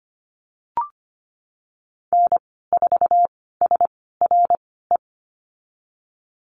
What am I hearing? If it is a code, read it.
Morse code, heard as a keyed tone, about 25 wpm, spaced as for 16 wpm, 700 Hz. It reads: N4SRE
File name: Lesson 15g - Number 5 - Call Signs 25wpm